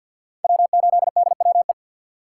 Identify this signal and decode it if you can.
Morse code, heard as a keyed tone, about 48 wpm, 700 Hz. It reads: W8DWE
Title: USA Call Signs - 5 Characters - 2x Character Spacing - 50wpm